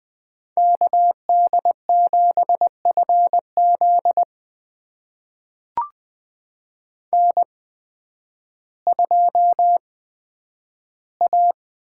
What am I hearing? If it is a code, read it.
Morse code, heard as a keyed tone, about 20 wpm, spaced as for 11 wpm, 700 Hz. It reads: KD7FZ N2A
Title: USA Call Signs - 5 Characters - 8x Character Spacing - 20wpm